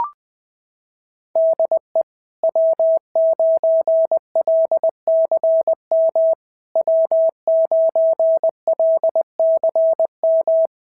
Morse code, heard as a keyed tone, about 22 wpm, 650 Hz. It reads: DE W9LCM W9LCM
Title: SOTA QSO Practice 20wpm